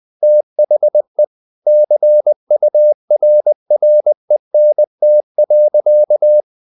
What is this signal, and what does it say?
Morse code, heard as a keyed tone, about 20 wpm, 600 Hz. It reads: THE CURRENT.
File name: the old man and the sea / the old man and the sea part 1 20wpm